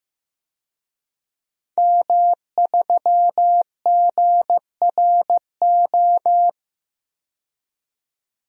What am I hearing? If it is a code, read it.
Morse code, heard as a keyed tone, about 15 wpm, 700 Hz. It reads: M3GRO
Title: DXCC-100 Call Signs - 5 Characters - 15wpm